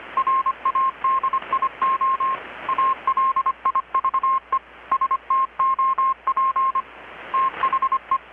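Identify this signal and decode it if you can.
Morse code, heard as a keyed tone, about 25 wpm, 1060 Hz. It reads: RADIO ALIVE STOP THE
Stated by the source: Morse from KPH's Night of Nights XX maritime radiotelegraphy event, received in Chiapas, Mexico about 3,800 km away on a Kenwood R-600 with a grounded 10 m long-wire antenna.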